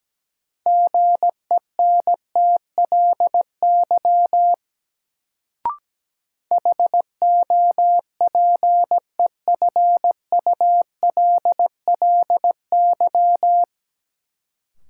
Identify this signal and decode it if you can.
Morse code, heard as a keyed tone, about 17 wpm, 700 Hz. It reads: GENTLY HOPEFULLY
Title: Sentences from Top 1500 Words - New Word Drill - 17wpm